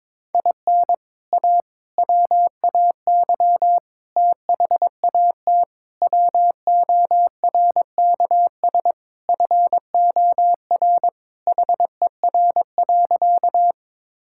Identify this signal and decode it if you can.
Morse code, heard as a keyed tone, about 22 wpm, 700 Hz. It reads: IN A WAY THAT WORKS FOR HER.
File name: practice / 22wpm / Sentences with 4-Word Phrases - 22wpm